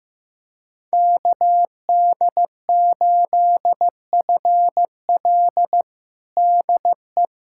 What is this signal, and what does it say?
Morse code, heard as a keyed tone, about 15 wpm, 700 Hz. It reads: KD8FL DE
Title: USA Callsign DE Callsign - 5 Characters - 15wpm